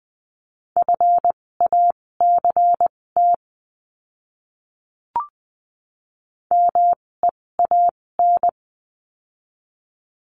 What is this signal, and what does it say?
Morse code, heard as a keyed tone, about 20 wpm, spaced as for 16 wpm, 700 Hz. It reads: FACT MEAN
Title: 256 Lesson 42a - Words 12-20wpm